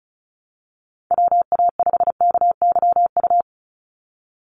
Morse code, heard as a keyed tone, about 36 wpm, 700 Hz. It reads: WA5KYU
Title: USA Call Signs 35wpm